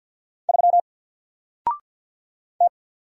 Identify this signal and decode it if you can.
Morse code, heard as a keyed tone, about 48 wpm, 700 Hz. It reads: 3 T
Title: Single Character (Letter:Number:Punctuation:Prosign) Alt 50wpm